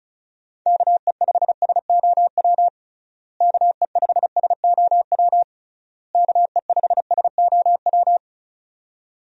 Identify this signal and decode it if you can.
Morse code, heard as a keyed tone, about 36 wpm, 700 Hz. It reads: KE5SOW KE5SOW KE5SOW
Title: USA Call Signs - 6 Characters - Repeated 3x - 35wpm